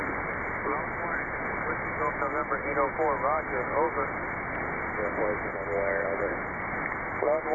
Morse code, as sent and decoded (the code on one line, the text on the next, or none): none